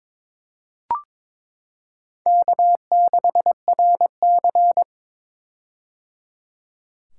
Morse code, heard as a keyed tone, about 21 wpm, 700 Hz. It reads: K6RC
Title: K1USN SST QSO Practice - 22wpm